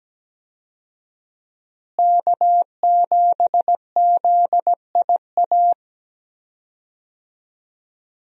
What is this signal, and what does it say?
Morse code, heard as a keyed tone, about 17 wpm, 700 Hz. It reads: K7ZIA
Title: USA Call Signs - 5 Characters - Repeated 2x - 17wpm